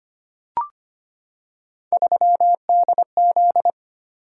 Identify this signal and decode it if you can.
Morse code, heard as a keyed tone, about 25 wpm, 700 Hz. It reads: 3DZ
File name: Three Letter-Number Combinations - 25wpm